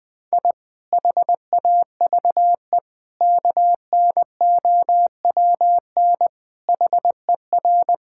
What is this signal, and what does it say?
Morse code, heard as a keyed tone, about 20 wpm, 700 Hz. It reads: I HAVE KNOWN HER
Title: Sentences with 5-Word Phrases - 20wpm